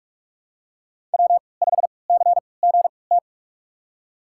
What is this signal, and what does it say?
Morse code, heard as a keyed tone, about 46 wpm, spaced as for 26 wpm, 700 Hz. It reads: W5CGT
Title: USA Call Signs - 5 Characters - 3x Character Spacing - 45wpm